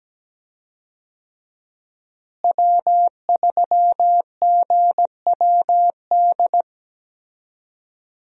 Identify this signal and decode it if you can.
Morse code, heard as a keyed tone, about 17 wpm, 700 Hz. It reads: W3GWD
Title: USA Call Signs - 5 Characters - Repeated 2x - 17wpm